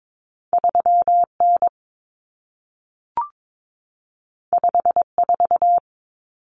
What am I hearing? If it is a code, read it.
Morse code, heard as a keyed tone, about 22 wpm, 700 Hz. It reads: 3N 54
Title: Common Cut Numbers - 2-Digit 22wpm